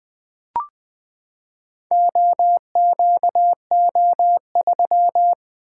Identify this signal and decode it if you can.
Morse code, heard as a keyed tone, about 21 wpm, 700 Hz. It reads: OQO3
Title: Four Letter-Number Combinations - 20wpm